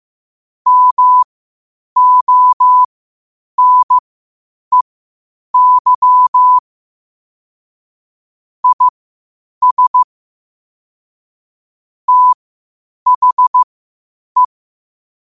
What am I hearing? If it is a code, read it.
Morse code, heard as a keyed tone, about 15 wpm, spaced as for 8 wpm, 1000 Hz. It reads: MONEY IS THE
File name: qotdfiles / 8wpm